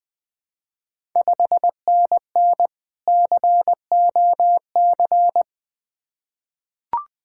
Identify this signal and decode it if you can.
Morse code, heard as a keyed tone, about 20 wpm, 690 Hz. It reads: 5NN COC
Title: Ontario QSO Party Exchange 20wpm